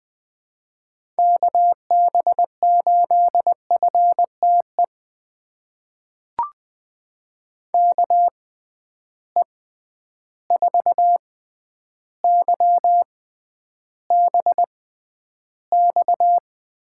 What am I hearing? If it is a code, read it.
Morse code, heard as a keyed tone, about 20 wpm, spaced as for 10 wpm, 700 Hz. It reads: KB8FTE KE4YBX
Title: USA Call Signs - 6 Characters - 6x Character Spacing - 20wpm